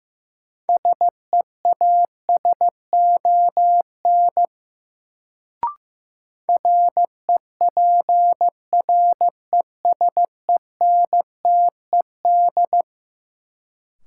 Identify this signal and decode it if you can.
Morse code, heard as a keyed tone, about 15 wpm, 700 Hz. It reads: SEASON REPRESENTED
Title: Sets of 5 Words - New Word Drill - 15wpm